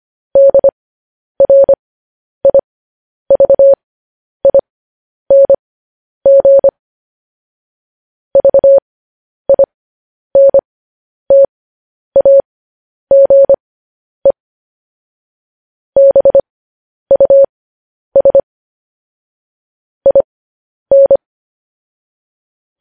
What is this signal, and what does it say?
Morse code, heard as a keyed tone, about 25 wpm, spaced as for 10 wpm, 550 Hz. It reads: DRIVING VINTAGE BUS IN
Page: news headlines in morse code at 10 wpm